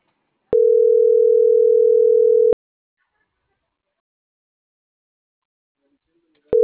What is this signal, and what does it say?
Ringback tone (North America), 440 + 480 Hz.